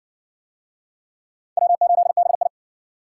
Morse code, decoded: W8BI

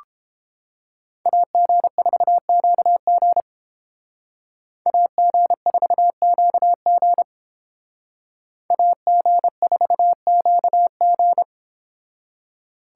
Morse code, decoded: AG4QG AG4QG AG4QG

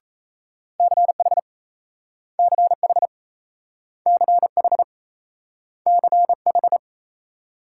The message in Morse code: CH CH CH CH

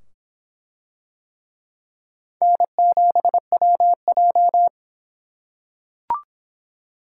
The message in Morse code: N7WJ

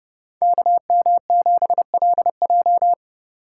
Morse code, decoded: KM7LJ

